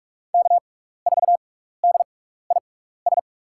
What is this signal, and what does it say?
Morse code, heard as a keyed tone, about 46 wpm, spaced as for 15 wpm, 700 Hz. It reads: K4DIS
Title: USA Call Signs - 5 Characters - 6x Character Spacing - 45wpm